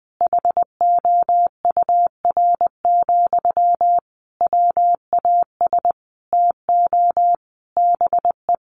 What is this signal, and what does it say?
Morse code, heard as a keyed tone, about 20 wpm, 700 Hz. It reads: HOUR, WAS TO BE
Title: Sentences from Top 1400 Words 20wpm